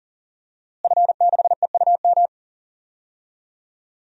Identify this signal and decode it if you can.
Morse code, heard as a keyed tone, about 40 wpm, 700 Hz. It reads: F6EUM
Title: DXCC-100 Call Signs - 5 Characters - Repeated 2x - 40wpm